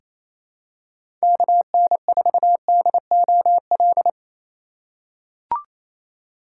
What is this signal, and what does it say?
Morse code, heard as a keyed tone, about 28 wpm, 700 Hz. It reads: KN4DOL
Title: USA Call Signs - 6 Characters - 2x Character Spacing - 28wpm